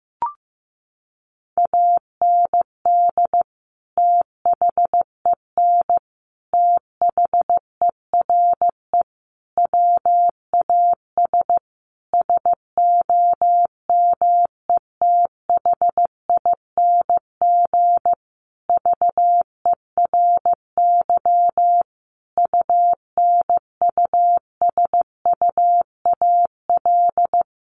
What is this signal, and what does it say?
Morse code, heard as a keyed tone, about 15 wpm, 700 Hz. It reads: AND THEN THERE WAS SOMETHING VERY UNUSUAL
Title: Sentences from Top 1400 Words 15wpm